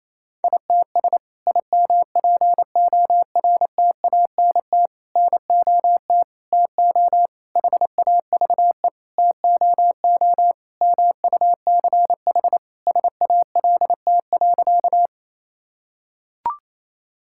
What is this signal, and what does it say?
Morse code, heard as a keyed tone, about 28 wpm, 700 Hz. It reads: ITS IMPORTANT NOT TO HAVE TOO MUCH SALT.
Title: Sentences from Top 1000 Words - Review Part 5 - 28wpm